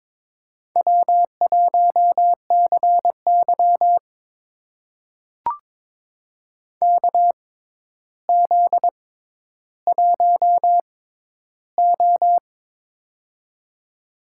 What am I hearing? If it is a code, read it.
Morse code, heard as a keyed tone, about 22 wpm, spaced as for 11 wpm, 700 Hz. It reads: W1CY KZ1O
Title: USA Call Signs - 4 Characters - 6x Character Spacing - 22wpm